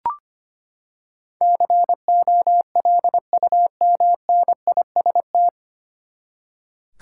Morse code, heard as a keyed tone, about 25 wpm, 700 Hz. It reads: COLUMNIST
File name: Nine Letter Words - 25wpm